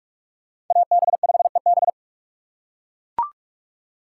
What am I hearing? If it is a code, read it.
Morse code, heard as a keyed tone, about 39 wpm, 700 Hz. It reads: AB5EB